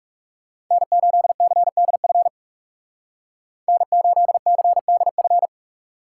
Morse code, heard as a keyed tone, about 37 wpm, 700 Hz. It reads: N8CDF N8CDF
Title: Speed Racing - USA 1x3 Call Signs 30wpm